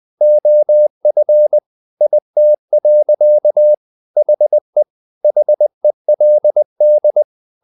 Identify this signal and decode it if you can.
Morse code, heard as a keyed tone, about 20 wpm, 600 Hz. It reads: OF IT. HE HELD
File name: the old man and the sea part 2 20wpm